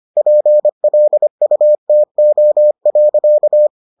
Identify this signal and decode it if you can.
Morse code, heard as a keyed tone, about 25 wpm, 600 Hz. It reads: PLUTO.